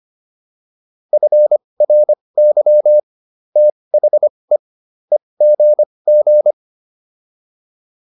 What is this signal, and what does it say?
Morse code, heard as a keyed tone, about 25 wpm, spaced as for 20 wpm, 600 Hz. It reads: FRY THE EGG